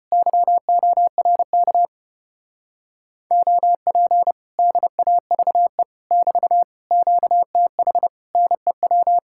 Morse code, heard as a keyed tone, about 33 wpm, 700 Hz. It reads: YORK OP DAVE <BT> QTH NEW